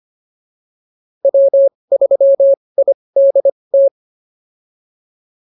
Morse code, heard as a keyed tone, about 25 wpm, spaced as for 20 wpm, 550 Hz. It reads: W3IDT